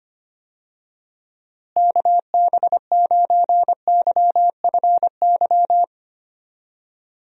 Morse code, heard as a keyed tone, about 25 wpm, 700 Hz. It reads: KB9YFY
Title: USA Call Signs - 6 Characters - Repeated 3x - 25wpm